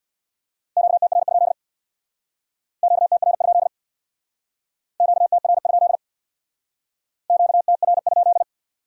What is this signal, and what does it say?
Morse code, heard as a keyed tone, about 61 wpm, 700 Hz. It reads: QTR? QTR? QTR? QTR?
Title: Speed-Racing Q-Codes - 50wpm